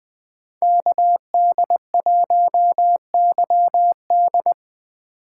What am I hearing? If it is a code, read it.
Morse code, heard as a keyed tone, about 20 wpm, 700 Hz. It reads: KD1YD
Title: USA Call Signs - 5 Characters - 3x Character Spacing - 20wpm